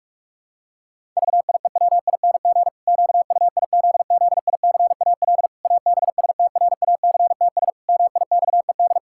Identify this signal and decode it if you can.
Morse code, heard as a keyed tone, about 46 wpm, 700 Hz. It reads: VIEWING QUIZZICAL ABSTRACTS MIXED
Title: Pangram Challenge v2 - 45wpm